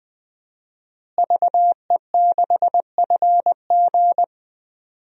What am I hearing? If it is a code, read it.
Morse code, heard as a keyed tone, about 20 wpm, 700 Hz. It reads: VE6FG
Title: DXCC-100 Call Signs - 5 Characters - Repeated 3x - 20wpm